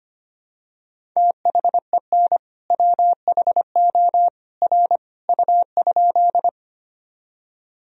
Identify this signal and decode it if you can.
Morse code, heard as a keyed tone, about 25 wpm, 700 Hz. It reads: THEN WHO R U?
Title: Sentences from Top 100 Words and Abbreviations 25wpm